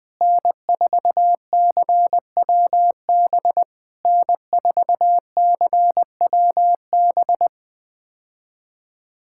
Morse code, heard as a keyed tone, about 20 wpm, 700 Hz. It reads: N4CWB N4CWB
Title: POTA QSO Practice - 20wpm